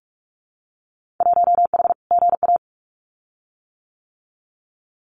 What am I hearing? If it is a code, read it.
Morse code, heard as a keyed tone, about 46 wpm, 700 Hz. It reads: 1H GA
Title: Winter Field Day Exchanges 45wpm